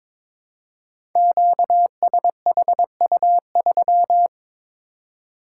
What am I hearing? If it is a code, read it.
Morse code, heard as a keyed tone, about 22 wpm, 700 Hz. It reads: QSHU3
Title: Code Groups - 5 Letter:Numbers - 22wpm